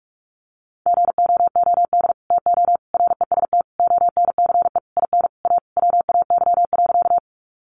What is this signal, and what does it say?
Morse code, heard as a keyed tone, about 44 wpm, 700 Hz. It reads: GOOD TO REST ONCE IN A WAY.
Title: Sentences from Top 500 Words 45wpm